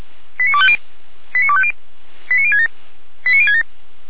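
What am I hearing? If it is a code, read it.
Five-tone selective call (ZVEI), heard as quick runs of five tones, tones of about 70 ms. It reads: 89250 89250 89067 89067
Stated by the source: ZVEI five-tone selective call 89240 received by SDR, Feuerwehr Orlamünde fire brigade alerting.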